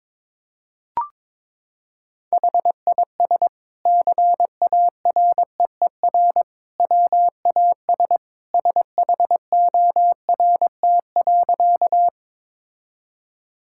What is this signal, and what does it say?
Morse code, heard as a keyed tone, about 22 wpm, 700 Hz. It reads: HIS CAREER WAS SHORT.
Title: Sentences from Top 1000 Words - Review Part 6 - 22wpm